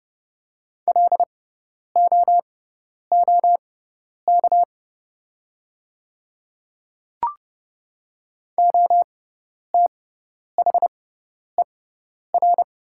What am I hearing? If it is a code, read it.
Morse code, heard as a keyed tone, about 30 wpm, spaced as for 10 wpm, 700 Hz. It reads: LOOK OTHER
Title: 6x Character Spacing - Top 300 Words - 30wpm